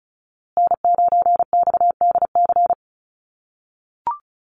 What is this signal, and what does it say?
Morse code, heard as a keyed tone, about 34 wpm, 700 Hz. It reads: N9XDC
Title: USA Call Signs Repeated 2x 35wpm